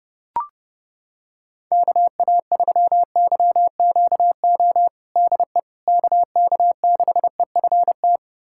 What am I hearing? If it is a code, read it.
Morse code, heard as a keyed tone, about 32 wpm, 700 Hz. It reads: KA3YQO DE KK6EFT
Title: USA Callsign DE Callsign - 6 Characters - 30wpm